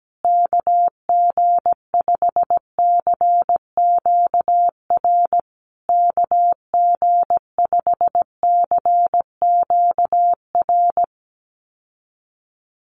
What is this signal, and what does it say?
Morse code, heard as a keyed tone, about 17 wpm, 700 Hz. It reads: KG5CQR KG5CQR
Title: USA Call Signs Repeated 3x 17wpm